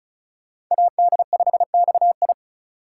Morse code, heard as a keyed tone, about 44 wpm, 700 Hz. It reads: AD5XI